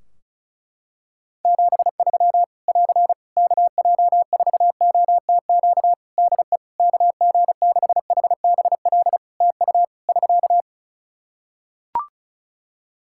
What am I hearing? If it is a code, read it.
Morse code, heard as a keyed tone, about 34 wpm, 700 Hz. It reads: 73 <AR> KJ4OTQ DE KG6HBL TU <SK>
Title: Practice QSO - Exchange 4 - Ending - 35wpm